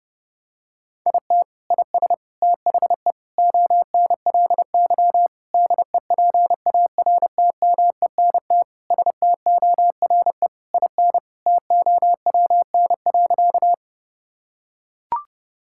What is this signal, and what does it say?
Morse code, heard as a keyed tone, about 32 wpm, 700 Hz. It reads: IT IS THE ONLY DEPARTMENT STORE IN TOWN.
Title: Sentences from Top 1000 Words - Review Part 5 - 30wpm